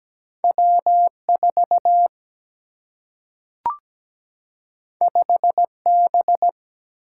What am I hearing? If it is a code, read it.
Morse code, heard as a keyed tone, about 17 wpm, 700 Hz. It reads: W4 5B
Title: Single-Letter Single-Number 17wpm